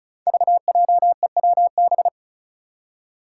Morse code, heard as a keyed tone, about 34 wpm, 700 Hz. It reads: VJEWB
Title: Five Challenging Letter-Number Combinations 35wpm